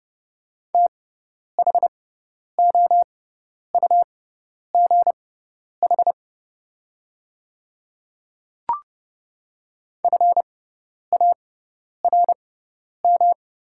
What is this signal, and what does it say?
Morse code, heard as a keyed tone, about 30 wpm, spaced as for 10 wpm, 690 Hz. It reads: THOUGH FARM
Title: 6x Character Spacing - Top 500 Words - 30wpm